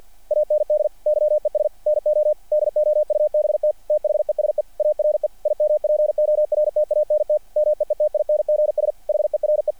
Morse code, heard as a keyed tone, about 48 wpm, 600 Hz. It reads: AND YES NO DOUBT THESE ARE IMPORTANT MEETINGS HERE
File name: archived / CRSnet